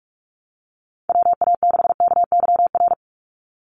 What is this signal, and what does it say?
Morse code, heard as a keyed tone, about 44 wpm, 700 Hz. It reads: WA6KYR
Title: NA SOTA Chaser Callsigns 45wpm